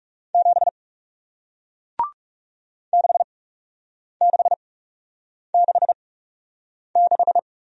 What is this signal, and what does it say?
Morse code, heard as a keyed tone, about 40 wpm, spaced as for 8 wpm, 700 Hz. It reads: 7 6666